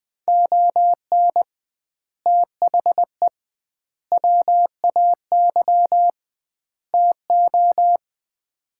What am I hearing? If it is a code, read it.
Morse code, heard as a keyed tone, about 20 wpm, 700 Hz. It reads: ON THE WAY TO
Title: 2x Word-Spacing - Sets of 4 Words - 20wpm